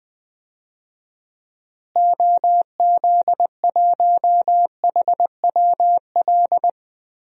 Morse code, heard as a keyed tone, about 20 wpm, 700 Hz. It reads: OZ1HWL
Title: DXCC-100 Call Signs - 6 Characters - Repeated 3x - 20wpm